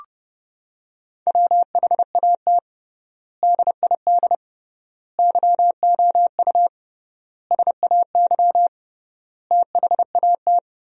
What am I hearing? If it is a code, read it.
Morse code, heard as a keyed tone, about 32 wpm, 700 Hz. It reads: WHAT DID YOU SAY THAT
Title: 3x Word-Spacing - Sentences from Top 200 Words - 30wpm